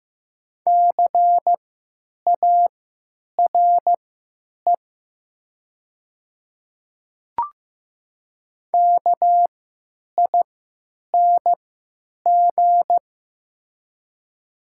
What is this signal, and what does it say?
Morse code, heard as a keyed tone, about 15 wpm, spaced as for 9 wpm, 700 Hz. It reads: CARE KING
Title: 3x Character Spacing - Top 500 Words - 15wpm